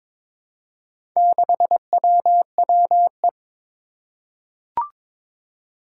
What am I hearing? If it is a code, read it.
Morse code, heard as a keyed tone, about 22 wpm, 700 Hz. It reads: 6WWE
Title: Four Challenging Letter-Number Combinations 22wpm